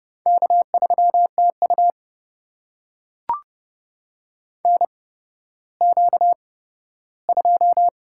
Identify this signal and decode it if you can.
Morse code, heard as a keyed tone, about 30 wpm, spaced as for 15 wpm, 700 Hz. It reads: K3TU NQ2
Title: USA Call Signs - 8x Character Spacing - 30wpm